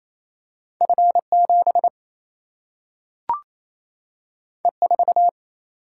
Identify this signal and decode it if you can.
Morse code, heard as a keyed tone, about 28 wpm, 700 Hz. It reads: F 7 E 4
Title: Single-Letter Single-Number 28wpm